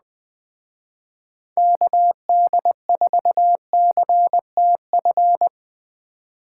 Morse code, heard as a keyed tone, about 21 wpm, 700 Hz. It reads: KD4CTF
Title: Speed Racing - USA 2x3 Call Signs 17wpm